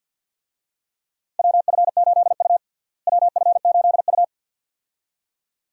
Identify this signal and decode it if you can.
Morse code, heard as a keyed tone, about 53 wpm, 700 Hz. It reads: WF8U WF8U
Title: USA Call Signs - 4 Characters - Repeated 2x - 50wpm